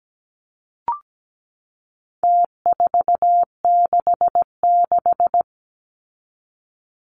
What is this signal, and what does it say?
Morse code, heard as a keyed tone, about 17 wpm, 700 Hz. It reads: T466
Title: Numbers - Contest Style - 4-Digit 17wpm